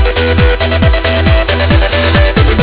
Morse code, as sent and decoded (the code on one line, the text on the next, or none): none